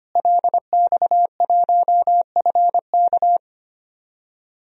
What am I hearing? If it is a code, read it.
Morse code, heard as a keyed tone, about 25 wpm, 700 Hz. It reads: LX1FK
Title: DXCC-100 Call Signs - 5 Characters - Repeated 2x - 25wpm